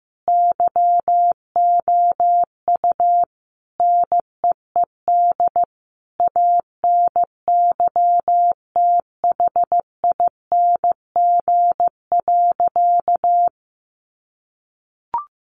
Morse code, heard as a keyed tone, about 14 wpm, 700 Hz. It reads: YOU NEED ANYTHING.
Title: Sentences from Top 1000 Words 15wpm